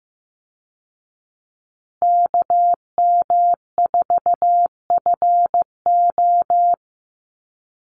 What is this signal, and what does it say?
Morse code, heard as a keyed tone, about 15 wpm, 700 Hz. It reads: KM4FO
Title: Call Signs Repeated 3x - 15wpm